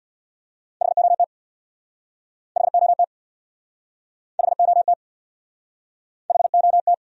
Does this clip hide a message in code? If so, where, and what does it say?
Yes: Morse code from 0.8 to 6.9 s, heard as a keyed tone, about 65 wpm, 700 Hz. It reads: HOT HOT HOT HOT